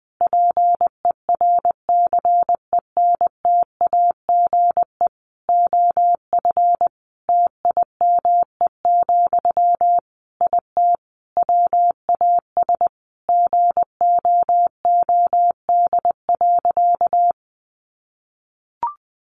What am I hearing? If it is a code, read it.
Morse code, heard as a keyed tone, about 20 wpm, 700 Hz. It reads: PERCENTAGE OF TIME, IT WAS GOOD.
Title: Sentences from Top 1300 Words 20wpm